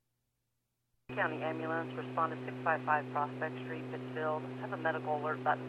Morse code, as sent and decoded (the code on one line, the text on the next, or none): none